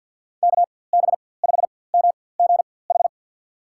Morse code, heard as a keyed tone, about 48 wpm, spaced as for 23 wpm, 700 Hz. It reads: KB5MGH
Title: USA Call Signs - 6 Characters - 4x Character Spacing - 50wpm